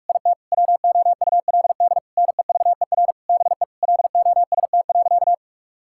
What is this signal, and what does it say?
Morse code, heard as a keyed tone, about 48 wpm, 700 Hz. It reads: IT WOULD NEVER BE LOST.
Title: Sentences from Top 1600 Words 45wpm